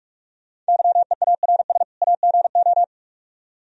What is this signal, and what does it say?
Morse code, heard as a keyed tone, about 44 wpm, 700 Hz. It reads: YEARS AGO